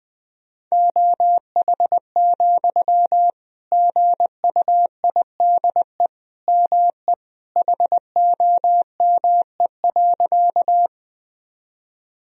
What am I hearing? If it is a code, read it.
Morse code, heard as a keyed tone, about 20 wpm, 700 Hz. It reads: OH, GUIDE ME HOME.
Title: Sentences from Top 1000 Words - Review Part 3 - 20wpm